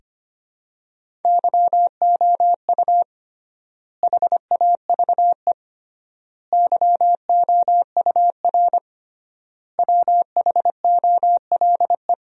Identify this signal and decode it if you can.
Morse code, heard as a keyed tone, about 25 wpm, 700 Hz. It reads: YOU HAVE YOUR WHOLE
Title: 3x Word-Spacing - Sentences from Top 400 Words - 25wpm